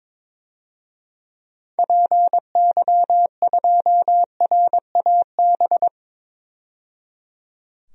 Morse code, heard as a keyed tone, about 22 wpm, 700 Hz. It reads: PY2RAB
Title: DXCC-100 Call Signs - 6 Characters - 22wpm